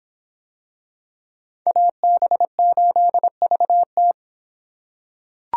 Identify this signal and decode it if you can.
Morse code, heard as a keyed tone, about 24 wpm, 700 Hz. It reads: AB8VT